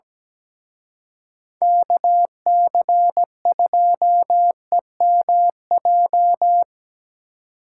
Morse code, heard as a keyed tone, about 20 wpm, 700 Hz. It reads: KC2EMJ
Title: Speed Racing - USA 2x3 Call Signs 17wpm